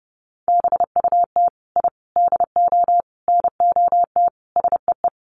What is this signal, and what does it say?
Morse code, heard as a keyed tone, about 30 wpm, 700 Hz. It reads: BUT I DO NOT SEE
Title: Sets of 5 Words Encore 30wpm